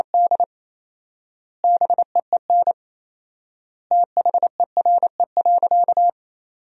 Morse code, heard as a keyed tone, about 28 wpm, 700 Hz. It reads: D BEEN THERE.
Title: 4x Word-Spacing - Sentences from Top 300 Words - 28wpm